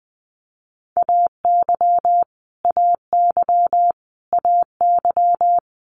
Morse code, heard as a keyed tone, about 20 wpm, 700 Hz. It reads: AY AY AY
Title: Multi-letter Phonemes Repeated 3x 20wpm